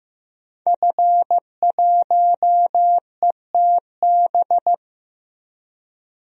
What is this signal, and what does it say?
Morse code, heard as a keyed tone, about 15 wpm, 700 Hz. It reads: F1ETB
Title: DXCC-100 Call Signs - 5 Characters - Repeated 3x - 15wpm